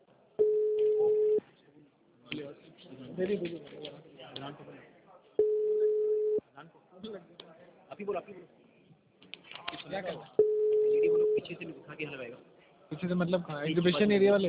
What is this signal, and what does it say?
Ringback tone (Europe, 425 Hz), with speech.